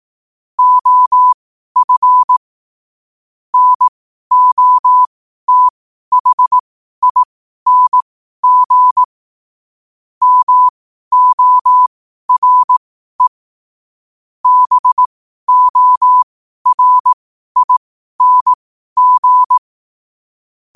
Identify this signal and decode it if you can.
Morse code, heard as a keyed tone, about 18 wpm, spaced as for 12 wpm, 1000 Hz. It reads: OF NOTHING MORE BORING
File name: qotdfiles / 12wpm